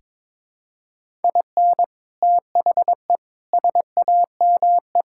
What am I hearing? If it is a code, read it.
Morse code, heard as a keyed tone, about 22 wpm, 700 Hz. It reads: IN THE SAME
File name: practice / 22wpm / Sentences with 4-Word Phrases - 22wpm